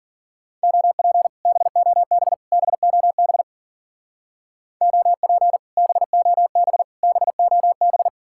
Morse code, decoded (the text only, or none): OP BOB BOB OP BOB BOB